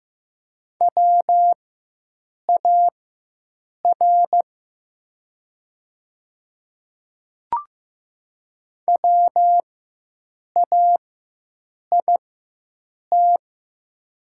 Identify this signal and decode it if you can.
Morse code, heard as a keyed tone, about 15 wpm, spaced as for 7 wpm, 690 Hz. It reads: WAR WAIT
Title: 4x Character Spacing - Top 500 Words - 15wpm